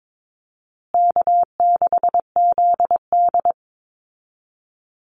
Morse code, heard as a keyed tone, about 22 wpm, 700 Hz. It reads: K6ZD
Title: Easy Call Signs Repeated 3x - 22wpm